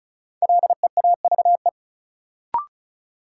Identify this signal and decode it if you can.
Morse code, heard as a keyed tone, about 36 wpm, 700 Hz. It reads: LEAVE
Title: Sentences from Top 400 Words - New Word Drill - 35wpm